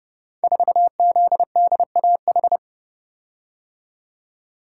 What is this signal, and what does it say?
Morse code, heard as a keyed tone, about 30 wpm, 700 Hz. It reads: 4ZDAH
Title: Code Groups - 5 Letter:Numbers - 30wpm